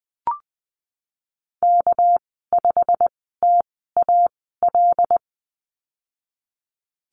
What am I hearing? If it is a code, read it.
Morse code, heard as a keyed tone, about 20 wpm, spaced as for 14 wpm, 700 Hz. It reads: K5TAL
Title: 231 Lesson 37g - Letter x - Call Signs 20wpm